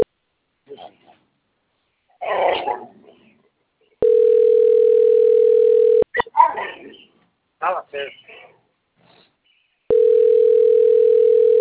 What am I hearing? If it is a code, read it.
Ringback tone (North America), 440 + 480 Hz, with speech.
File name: rg-1600-unknown- - - .90098